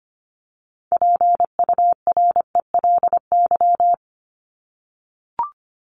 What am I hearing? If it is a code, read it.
Morse code, heard as a keyed tone, about 25 wpm, 700 Hz. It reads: PURELY
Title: LY Suffix - Top 100 Words with Base Words 25wpm